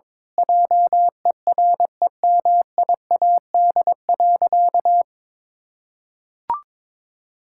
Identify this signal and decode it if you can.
Morse code, heard as a keyed tone, about 21 wpm, 700 Hz. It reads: JEREMIAD.
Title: Pangram Challenge v2 - 22wpm